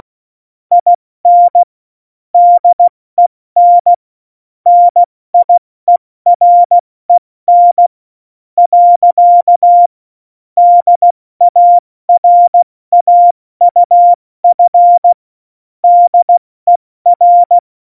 Morse code, heard as a keyed tone, about 16 wpm, 700 Hz. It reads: IN DEN NIEREN . DARAUF DER
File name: Morsekurs / Witze / Witz 20. Februar 2018, 70 F80